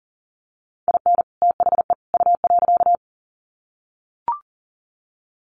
Morse code, heard as a keyed tone, about 40 wpm, 700 Hz. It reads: IN THE U.
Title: Sets of 3 Words 40wpm